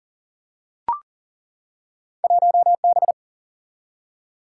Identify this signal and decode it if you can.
Morse code, heard as a keyed tone, about 38 wpm, 700 Hz. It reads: 1 B
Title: Single-Letter Single-Number 40wpm